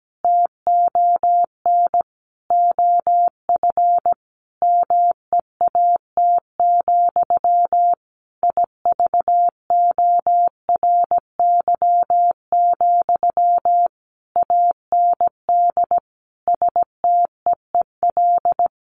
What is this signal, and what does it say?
Morse code, heard as a keyed tone, about 17 wpm, 700 Hz. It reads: TON OF MEAT, IVORY, AND STEEL
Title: A Sound Of Thunder 17wpm